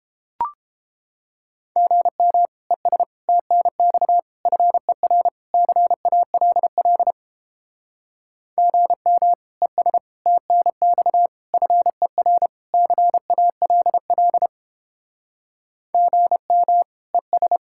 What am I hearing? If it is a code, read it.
Morse code, heard as a keyed tone, about 29 wpm, 700 Hz. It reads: GM ES TNX FER CALL GM ES TNX FER CALL GM ES